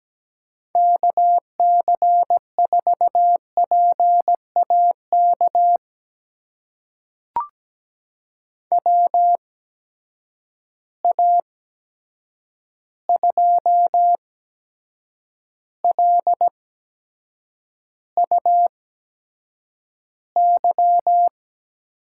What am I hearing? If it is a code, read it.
Morse code, heard as a keyed tone, about 17 wpm, spaced as for 7 wpm, 700 Hz. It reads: KC4PAK WA2LUY
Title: USA Call Signs - 6 Characters - 8x Character Spacing - 17wpm